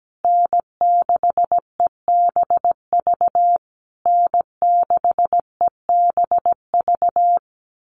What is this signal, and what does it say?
Morse code, heard as a keyed tone, about 17 wpm, 700 Hz. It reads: N6EBV N6EBV
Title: USA Call Signs Repeated 3x 17wpm